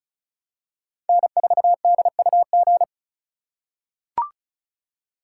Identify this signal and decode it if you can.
Morse code, heard as a keyed tone, about 36 wpm, 700 Hz. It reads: N4DUG
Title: USA Call Signs - 2x Character Spacing - 35wpm